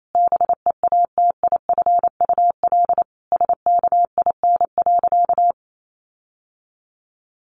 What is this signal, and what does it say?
Morse code, heard as a keyed tone, about 28 wpm, 700 Hz. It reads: BEATIFUL SKIN.
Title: Sentences from Top 1300 Words 28wpm